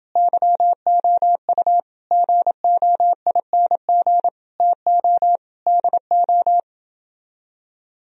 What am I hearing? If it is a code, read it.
Morse code, heard as a keyed tone, about 29 wpm, 700 Hz. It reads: YOU GOING TO DO